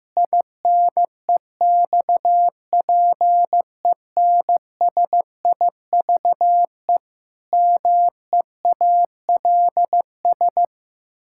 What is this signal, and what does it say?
Morse code, heard as a keyed tone, about 14 wpm, 700 Hz. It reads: INEXPENSIVE MEALS
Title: Pangram Challenge v2 - 15wpm